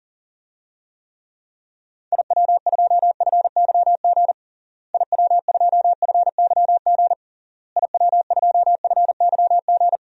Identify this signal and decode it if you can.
Morse code, heard as a keyed tone, about 40 wpm, 700 Hz. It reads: IW2FYG IW2FYG IW2FYG
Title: DXCC-100 Call Signs - 6 Characters - Repeated 3x - 40wpm